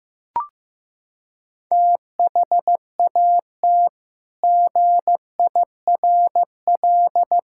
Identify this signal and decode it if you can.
Morse code, heard as a keyed tone, about 15 wpm, 700 Hz. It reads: THAT GIRL
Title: Sentences from Top 1600 Words 15wpm